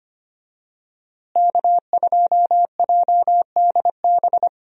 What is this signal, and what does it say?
Morse code, heard as a keyed tone, about 25 wpm, 700 Hz. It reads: K2JDB
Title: USA Call Signs - 5 Characters - Repeated 3x - 25wpm